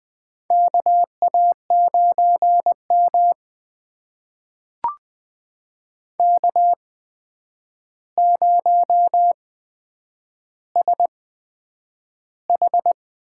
Morse code, heard as a keyed tone, about 20 wpm, spaced as for 9 wpm, 700 Hz. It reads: KA9M K0SH
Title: USA Call Signs - 4 Characters - 8x Character Spacing - 20wpm